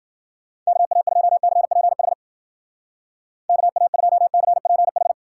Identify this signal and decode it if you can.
Morse code, heard as a keyed tone, about 52 wpm, 700 Hz. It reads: KA2CPH KA2CPH